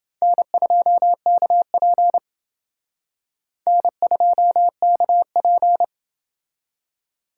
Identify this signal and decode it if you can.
Morse code, heard as a keyed tone, about 28 wpm, 700 Hz. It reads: N2KP N2KP